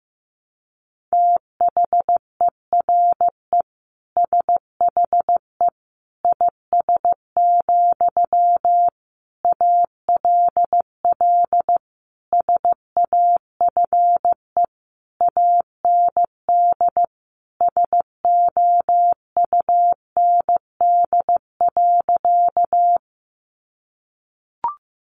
Morse code, heard as a keyed tone, about 15 wpm, 700 Hz. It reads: THERE SHE IS, ALL SAFE AND SOUND.
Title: Sentences from Top 900 Words 15wpm